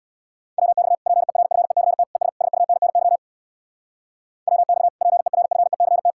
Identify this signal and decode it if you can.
Morse code, heard as a keyed tone, about 64 wpm, 700 Hz. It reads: WX PERFECT ES SUNNY WX PERFECT